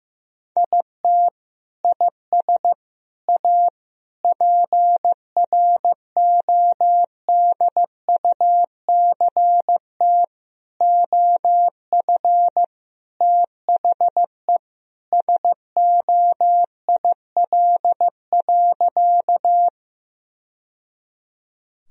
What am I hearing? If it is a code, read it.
Morse code, heard as a keyed tone, about 15 wpm, 700 Hz. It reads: IT IS A PRODUCT OF THE SOIL.
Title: Sentences from Top 1000 Words - Review Part 4 - 15wpm